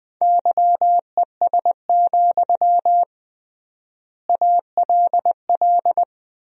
Morse code, heard as a keyed tone, about 20 wpm, 700 Hz. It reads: YES, ALL
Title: 3x Word-Spacing - Sentences from Top 500 Words - 20wpm